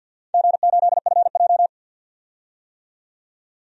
Morse code, heard as a keyed tone, about 52 wpm, 700 Hz. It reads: G8FJ
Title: DXCC-100 Call Signs - 4 Characters - 50wpm